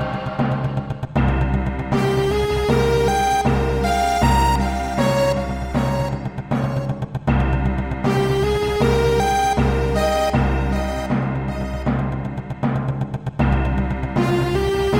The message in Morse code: none